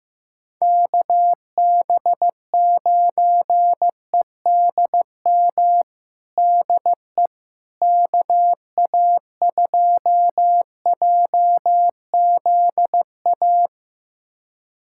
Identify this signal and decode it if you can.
Morse code, heard as a keyed tone, about 15 wpm, 700 Hz. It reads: KB9EDM DE KA2JZA
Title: USA Callsign DE Callsign 15wpm